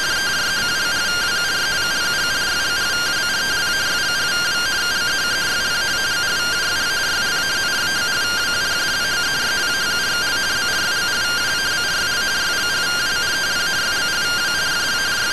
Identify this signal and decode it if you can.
Two-tone teleprinter signal, about 170 Hz shift.